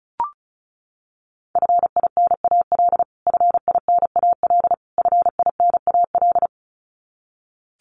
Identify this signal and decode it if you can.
Morse code, heard as a keyed tone, about 34 wpm, 700 Hz. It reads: FINAL FINAL FINAL
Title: Top 500 Words Repeated 35wpm